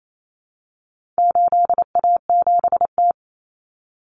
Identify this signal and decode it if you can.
Morse code, heard as a keyed tone, about 28 wpm, 700 Hz. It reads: 8A7T
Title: Common Cut Numbers - 4-Digit 28wpm